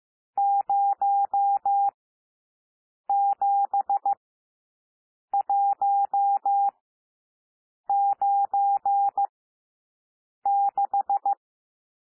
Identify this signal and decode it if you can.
Morse code, heard as a keyed tone, about 15 wpm, spaced as for 6 wpm, 800 Hz. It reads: 07196
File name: SOUND NUMBERS 15 wpm